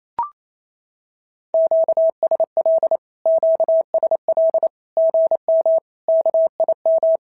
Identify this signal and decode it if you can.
Morse code, heard as a keyed tone, about 28 wpm, 650 Hz. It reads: QSL QSL GM KIM
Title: SOTA QSO Practice 28wpm